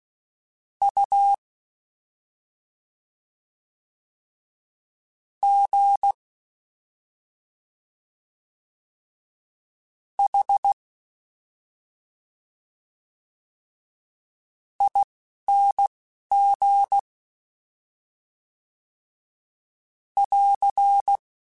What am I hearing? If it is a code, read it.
Morse code, heard as a keyed tone, about 16 wpm, spaced as for 3 wpm, 800 Hz. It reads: UGHING <AR>
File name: Special INGs-mp3s / Special INGs-mp3s / Slow ing @10